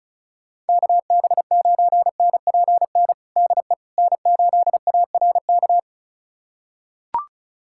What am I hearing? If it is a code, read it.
Morse code, heard as a keyed tone, about 36 wpm, 700 Hz. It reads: KB9NPN DE N8ARK
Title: USA Callsign DE Callsign 35wpm